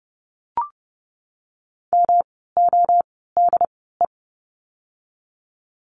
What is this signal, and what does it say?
Morse code, heard as a keyed tone, about 30 wpm, spaced as for 17 wpm, 700 Hz. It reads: MODE
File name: course / 30:10wpm / 108 Lesson 18f - Letter m - Words 30wpm